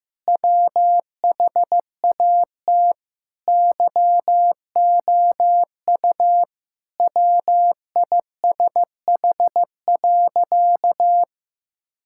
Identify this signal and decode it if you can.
Morse code, heard as a keyed tone, about 15 wpm, 700 Hz. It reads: WHAT YOU WISH.
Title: Sentences from Top 1600 Words 15wpm